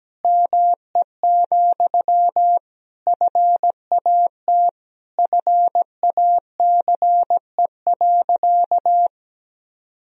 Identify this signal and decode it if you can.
Morse code, heard as a keyed tone, about 17 wpm, 700 Hz. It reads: ME, FAT FACE.
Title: Sentences from Top 1000 Words - Review Part 4 - 17wpm